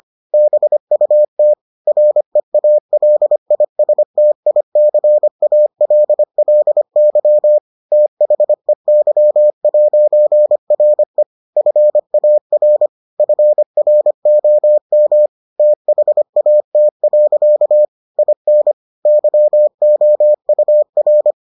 Morse code, heard as a keyed tone, about 25 wpm, 600 Hz. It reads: BUT REALISTICALLY THEY'RE FAR FROM THAT. IN YOUR